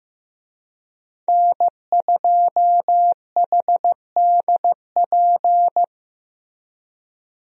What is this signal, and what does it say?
Morse code, heard as a keyed tone, about 15 wpm, 700 Hz. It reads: N2HDP
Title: USA Call Signs - 5 Characters - Repeated 2x - 15wpm